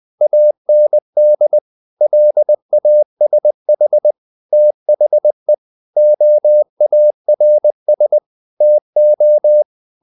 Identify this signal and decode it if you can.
Morse code, heard as a keyed tone, about 20 wpm, 600 Hz. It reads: AND LASH THE OARS TO
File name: the old man and the sea part 2 20wpm